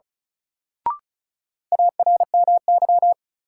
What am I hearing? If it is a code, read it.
Morse code, heard as a keyed tone, about 36 wpm, 700 Hz. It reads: ARMY
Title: Sentences from Top 800 Words - New Word Drill - 35wpm